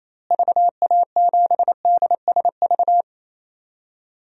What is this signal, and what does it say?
Morse code, heard as a keyed tone, about 28 wpm, 700 Hz. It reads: VA7DSV
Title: DXCC-100 Call Signs - 6 Characters - Repeated 3x - 28wpm